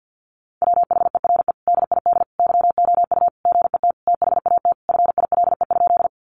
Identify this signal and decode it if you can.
Morse code, heard as a keyed tone, about 50 wpm, 700 Hz. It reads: WHERE DID YOU GET THAT FILE?
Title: Sentences from Top 1200 Words 50wpm